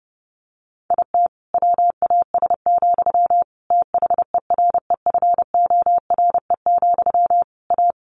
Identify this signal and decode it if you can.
Morse code, heard as a keyed tone, about 30 wpm, 700 Hz. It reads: IT WAS, THEREFORE, A
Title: Sentences from Top 1400 Words 30wpm